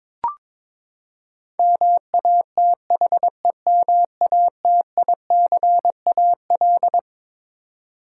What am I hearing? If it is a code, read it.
Morse code, heard as a keyed tone, about 22 wpm, 700 Hz. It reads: MATHEMATICAL
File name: Twelve Letter Words - 22wpm